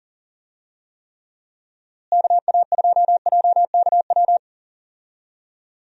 Morse code, heard as a keyed tone, about 39 wpm, 700 Hz. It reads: KA2JKW
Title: USA Call Signs - 6 Characters - Repeated 3x - 40wpm